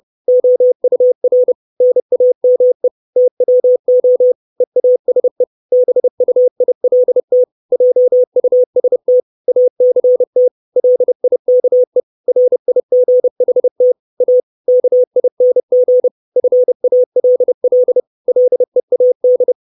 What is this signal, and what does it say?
Morse code, heard as a keyed tone, about 30 wpm, 500 Hz. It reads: OUR NAME TWO EASE BUILT JUST ACT LIKE RIGHT A KING FALL LEAD
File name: 632 Words 30WPM